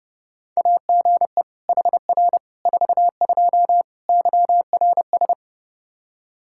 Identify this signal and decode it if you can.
Morse code, heard as a keyed tone, about 30 wpm, 700 Hz. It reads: AGE HR 42 YRS
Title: Practice QSO - Exchange 3 - More Info - 30wpm